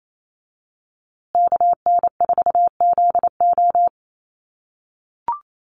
Morse code, heard as a keyed tone, about 28 wpm, 700 Hz. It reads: KN4ZO
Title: USA Call Signs Repeated 2x 28wpm